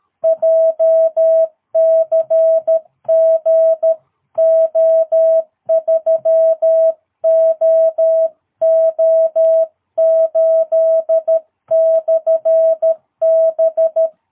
Morse code, heard as a keyed tone, about 13 wpm, 660 Hz. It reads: JCG O3OO8/B
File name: jh8cbh / 2022cw